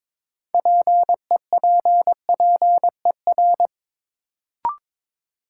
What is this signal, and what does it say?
Morse code, heard as a keyed tone, about 22 wpm, 700 Hz. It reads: PEPPER
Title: Sets of 5 Words - New Word Drill - 22wpm